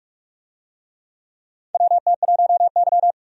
Morse code, heard as a keyed tone, about 46 wpm, 700 Hz. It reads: WT1Y
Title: USA Call Signs - 4 Characters - 45wpm